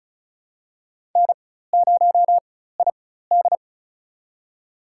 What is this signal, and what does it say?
Morse code, heard as a keyed tone, about 36 wpm, spaced as for 16 wpm, 700 Hz. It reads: N0ID
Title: USA Call Signs - 4x Character Spacing - 35wpm